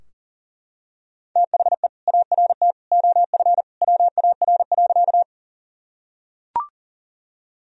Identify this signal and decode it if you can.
Morse code, heard as a keyed tone, about 40 wpm, 700 Hz. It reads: THE ART OF WAR.
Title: Sentences from Top 1000 Words - Review Part 3 - 40wpm